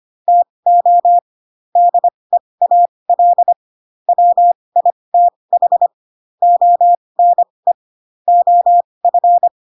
Morse code, heard as a keyed tone, about 25 wpm, spaced as for 20 wpm, 700 Hz. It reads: TO DEAL WITH ONE OF
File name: Pinocchio / Pinocchio ch6 15 25 ik1hge